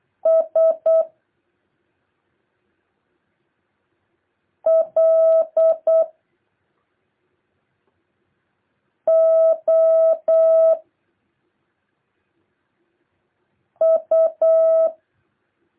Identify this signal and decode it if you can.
Morse code, heard as a keyed tone, about 8 wpm, 650 Hz. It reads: S L O U
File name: jh8cbh / 2023cw